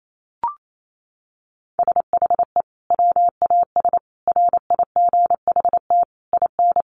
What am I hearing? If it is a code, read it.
Morse code, heard as a keyed tone, about 28 wpm, 700 Hz. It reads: SHE WAS RIGHT IN
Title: Sentences from Top 1500 Words 28wpm